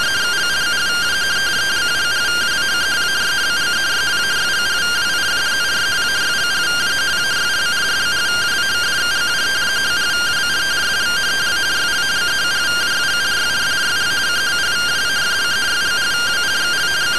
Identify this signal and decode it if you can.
Two-tone teleprinter signal, about 170 Hz shift.